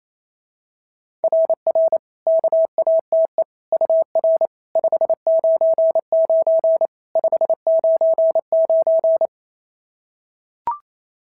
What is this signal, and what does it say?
Morse code, heard as a keyed tone, about 27 wpm, 650 Hz. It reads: RR KATE UR 599 599
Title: SOTA QSO Practice 28wpm